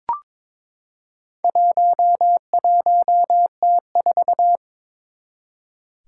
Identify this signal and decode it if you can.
Morse code, heard as a keyed tone, about 23 wpm, 700 Hz. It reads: 11T4
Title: Four Challenging Letter-Number Combinations 22wpm